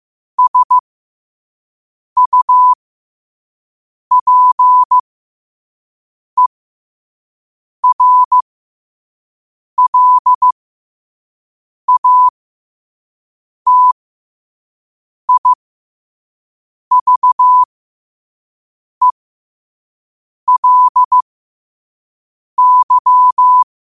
Morse code, heard as a keyed tone, about 15 wpm, spaced as for 5 wpm, 1000 Hz. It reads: SUPERLATIVELY
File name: qotdfiles / 5wpm